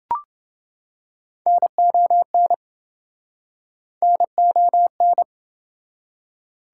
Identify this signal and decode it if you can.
Morse code, heard as a keyed tone, about 26 wpm, 700 Hz. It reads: NON NON